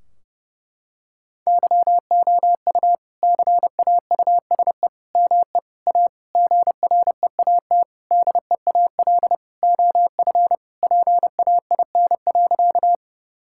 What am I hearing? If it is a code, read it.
Morse code, heard as a keyed tone, about 30 wpm, 700 Hz. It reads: YOU CAUSE ME A GREAT DEAL OF PAIN.
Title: Sentences from Top 1600 Words 30wpm